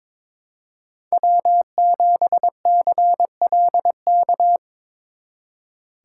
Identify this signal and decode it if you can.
Morse code, heard as a keyed tone, about 26 wpm, 700 Hz. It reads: W7CLK
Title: Speed Racing - USA 1x3 Call Signs 22wpm